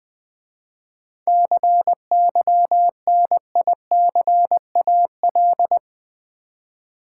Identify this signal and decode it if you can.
Morse code, heard as a keyed tone, about 20 wpm, 700 Hz. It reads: CYNICAL
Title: 7 Letter Words - First 3 Letters Given - 20wpm